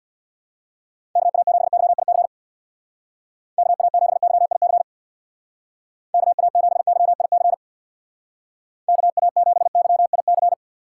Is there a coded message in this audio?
Yes: Morse code from 1.2 to 10.5 s, heard as a keyed tone, about 66 wpm, 700 Hz. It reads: KA7YIC KA7YIC KA7YIC KA7YIC